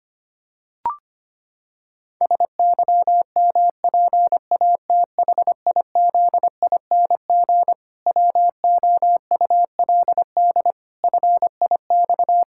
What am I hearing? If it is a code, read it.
Morse code, heard as a keyed tone, about 25 wpm, 700 Hz. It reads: SYMPATHIZING WOULD FIX
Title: Pangram Challenge v2 - 25wpm